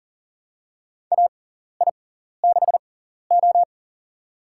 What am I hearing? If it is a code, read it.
Morse code, heard as a keyed tone, about 39 wpm, spaced as for 14 wpm, 690 Hz. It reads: AI6O
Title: USA Call Signs - 6x Character Spacing - 40wpm